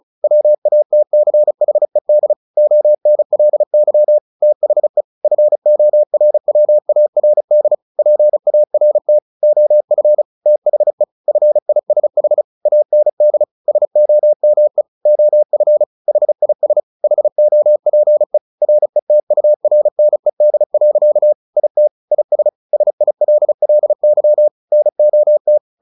Morse code, heard as a keyed tone, about 34 wpm, 600 Hz. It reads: WATCHED ONLY THE FORWARD PART OF THE FISH AND SOME OF HIS HOPE RETURNED. IT IS SILLY NOT
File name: the old man and the sea 2 35wpm